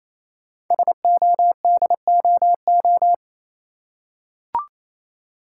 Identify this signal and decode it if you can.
Morse code, heard as a keyed tone, about 28 wpm, 700 Hz. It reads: SODOO
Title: Code Groups - 5 Letters - 28wpm